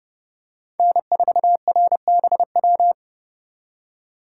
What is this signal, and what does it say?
Morse code, heard as a keyed tone, about 30 wpm, 700 Hz. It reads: N4RBW